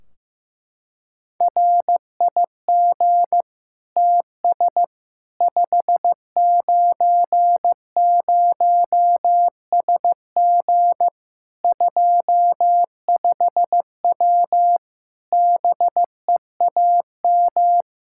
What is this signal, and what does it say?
Morse code, heard as a keyed tone, about 16 wpm, 700 Hz. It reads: RIG TS 590SG 25W BEAM
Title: Practice QSO - Exchange 2 - Info - 15wpm